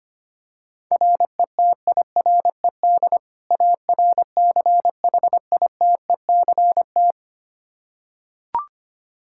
Morse code, heard as a keyed tone, about 25 wpm, 700 Hz. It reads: RETIRED ARCHITECT
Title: Practice QSO - Exchange 3 - More Info - 25wpm